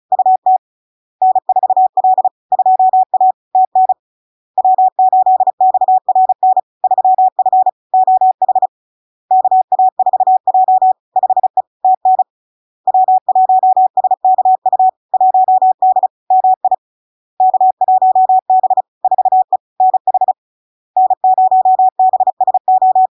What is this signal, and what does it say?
Morse code, heard as a keyed tone, about 36 wpm, 750 Hz. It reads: UT N4L 2A TN W8XRN 3F OH KA4J 5E TN W1SKU 1D MI K1B 4E NH N0BSO
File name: LARK-FD-2022-Log-in-CW-35wpm-750-Hz